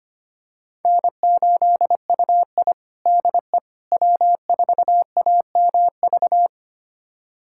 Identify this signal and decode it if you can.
Morse code, heard as a keyed tone, about 25 wpm, 700 Hz. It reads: N8UI DE W4AMV
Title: USA Callsign DE Callsign 25wpm